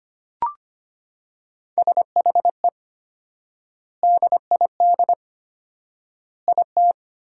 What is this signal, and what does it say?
Morse code, heard as a keyed tone, about 25 wpm, 700 Hz. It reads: SHE DID IT
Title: 4x Word-Spacing - Sentences from Top 400 Words - 25wpm